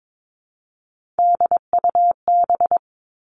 Morse code, heard as a keyed tone, about 22 wpm, 700 Hz. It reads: DUB
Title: Three Challenging Letter-Number Combinations 22wpm